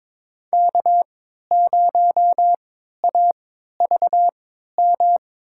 Morse code, heard as a keyed tone, about 22 wpm, spaced as for 13 wpm, 700 Hz. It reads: K0AVM
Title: USA Call Signs - 5 Characters - 3x Character Spacing - 22wpm